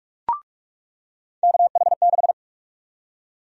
Morse code, heard as a keyed tone, about 46 wpm, 700 Hz. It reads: KH6
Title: USA Call Sign Prefixes - 45wpm